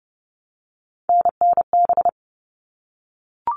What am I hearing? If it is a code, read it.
Morse code, heard as a keyed tone, about 30 wpm, 700 Hz. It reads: NNB
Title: All Cut Numbers - 3-Digit 30wpm